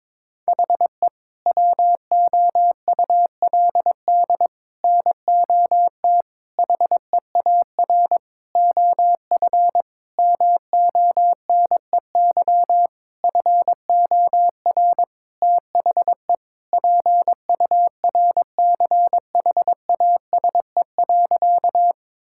Morse code, heard as a keyed tone, about 22 wpm, 700 Hz. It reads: HE WOULD NOT HEAR OF MONEY FOR THE PURCHASE.
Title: Sentences from Top 1000 Words - Review Part 5 - 22wpm